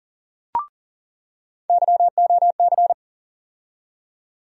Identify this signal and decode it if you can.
Morse code, heard as a keyed tone, about 40 wpm, 700 Hz. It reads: YOC
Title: USA Call Sign Postfixes - 3 Letters 40wpm